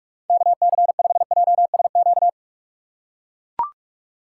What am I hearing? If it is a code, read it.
Morse code, heard as a keyed tone, about 42 wpm, 700 Hz. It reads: KC5JSQ
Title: USA Call Signs - 6 Characters - 6x Character Spacing - 45wpm